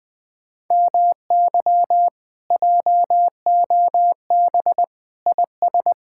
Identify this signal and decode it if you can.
Morse code, heard as a keyed tone, about 20 wpm, 700 Hz. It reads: MY JOB IS
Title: Speed-Racing Common QSO Phrases - 20wpm